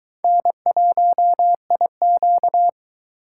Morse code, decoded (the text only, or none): N1IQ